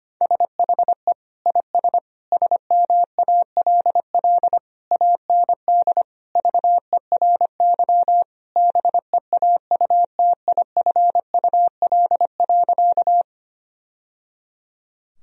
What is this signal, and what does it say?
Morse code, heard as a keyed tone, about 25 wpm, 700 Hz. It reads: SHE IS SMALL AND VERY BEAUTIFUL.
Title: Sentences from Top 1000 Words - Review Part 1 - 25wpm